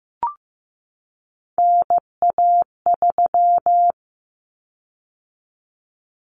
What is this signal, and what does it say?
Morse code, heard as a keyed tone, about 15 wpm, 700 Hz. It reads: NA3
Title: Common Cut Numbers - 3-Digit 15wpm